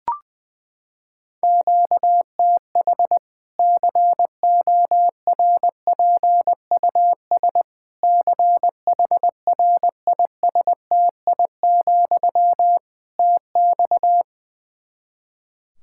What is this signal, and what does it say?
Morse code, heard as a keyed tone, about 20 wpm, 700 Hz. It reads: QTH CORPUS CHRISTI, TX
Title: K1USN SST QSO Practice 20wpm